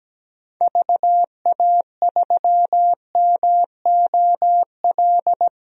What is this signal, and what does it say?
Morse code, heard as a keyed tone, about 17 wpm, 700 Hz. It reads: VA3MOL
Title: DXCC-100 Call Signs - 6 Characters - 17wpm